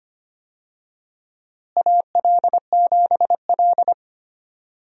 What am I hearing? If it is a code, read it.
Morse code, heard as a keyed tone, about 25 wpm, 700 Hz. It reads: AL7L
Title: USA Call Signs - 4 Characters - Repeated 3x - 25wpm